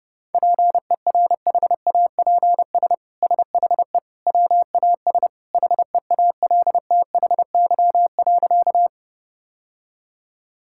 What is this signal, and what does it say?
Morse code, heard as a keyed tone, about 32 wpm, 700 Hz. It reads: PERHAPS SHE WAS HEALTHY.
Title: Sentences from Top 1000 Words - Review Part 5 - 30wpm